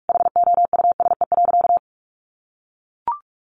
Morse code, heard as a keyed tone, about 44 wpm, 700 Hz. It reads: HOUSE.
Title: Sentences from Top 1400 Words 45wpm